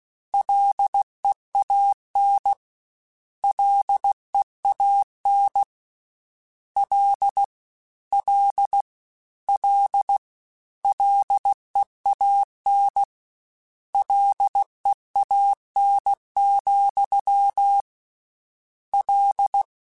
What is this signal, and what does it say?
Morse code, heard as a keyed tone, about 16 wpm, 800 Hz. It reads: LEAN LEAN L L L LEAN LEAN, L
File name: Combinations / Combinations-mp3s / Combinations@15